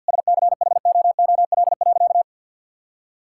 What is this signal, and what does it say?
Morse code, heard as a keyed tone, about 53 wpm, 690 Hz. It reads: SCHOOL.
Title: Sentences from Top 1000 Words - Review Part 5 - 50wpm